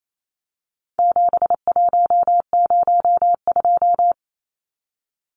Morse code, heard as a keyed tone, about 28 wpm, 700 Hz. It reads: 7102
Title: Numbers - 4-Digit 28wpm